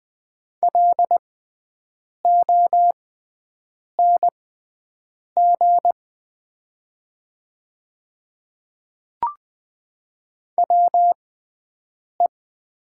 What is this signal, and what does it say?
Morse code, heard as a keyed tone, about 20 wpm, spaced as for 7 wpm, 700 Hz. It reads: LONG WE
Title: 6x Character Spacing - Top 100 Words - 20wpm